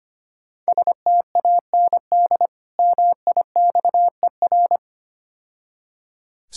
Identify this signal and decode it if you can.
Morse code, heard as a keyed tone, about 26 wpm, 700 Hz. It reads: STAND MIXER
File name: Easy Words - Speed-Racing with Hints - 25wpm